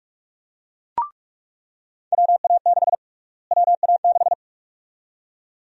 Morse code, heard as a keyed tone, about 46 wpm, 700 Hz. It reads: WA6 WA6
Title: USA Call Sign Prefixes - Repeated 2x - 45wpm